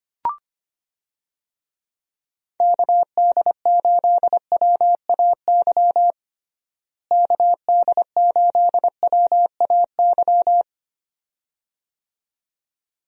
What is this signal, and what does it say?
Morse code, heard as a keyed tone, about 25 wpm, 700 Hz. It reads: KD8WAY KD8WAY
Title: USA Call Signs - 6 Characters - Repeated 2x - 25wpm